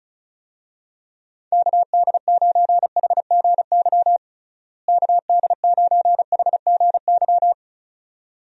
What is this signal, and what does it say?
Morse code, heard as a keyed tone, about 35 wpm, 700 Hz. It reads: KD9HGY KD9HGY
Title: USA Call Signs - 6 Characters - Repeated 2x - 35wpm